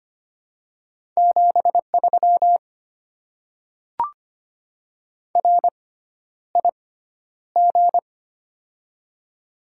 Morse code, heard as a keyed tone, about 25 wpm, spaced as for 11 wpm, 700 Hz. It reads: 73 RIG
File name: Top 90 QSO Elements - 6x Character Spacing - 25wpm